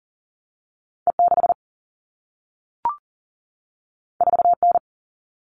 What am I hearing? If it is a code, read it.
Morse code, heard as a keyed tone, about 42 wpm, 700 Hz. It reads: E6 4N
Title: All Cut Numbers - 2-Digit 40wpm